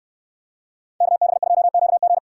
Morse code, heard as a keyed tone, about 69 wpm, 700 Hz. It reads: KB2QG